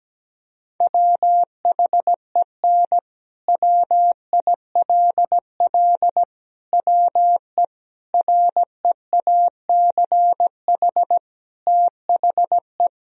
Morse code, heard as a keyed tone, about 17 wpm, 700 Hz. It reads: WHEN WILL WE REACH THE
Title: Sentences from Top 1000 Words - Review Part 4 - 17wpm